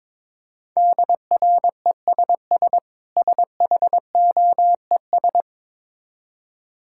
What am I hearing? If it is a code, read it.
Morse code, heard as a keyed tone, about 22 wpm, 690 Hz. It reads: DRESS SHOES
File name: Difficult Words with Hints - 22wpm